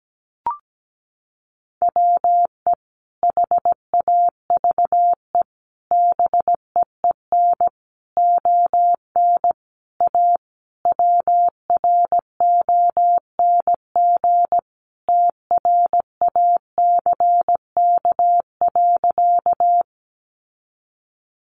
Morse code, heard as a keyed tone, about 17 wpm, 700 Hz. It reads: WE HAVE BEEN ON A WRONG TRACK.
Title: Sentences from Top 1100 Words 17wpm